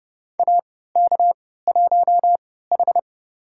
Morse code, heard as a keyed tone, about 32 wpm, spaced as for 17 wpm, 700 Hz. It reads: AK1H
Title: USA Call Signs - 3x Character Spacing - 30wpm